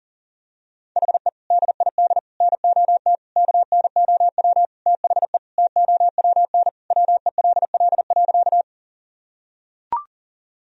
Morse code, heard as a keyed tone, about 40 wpm, 700 Hz. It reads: HE DID NOT KNOW THE TOWN WELL.
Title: Sentences from Top 1000 Words - Review Part 2 - 40wpm